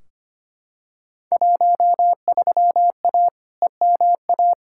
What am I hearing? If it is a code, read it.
Morse code, heard as a keyed tone, about 25 wpm, 700 Hz. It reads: 13A EMA
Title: Field Day Exchanges Encore - 25wpm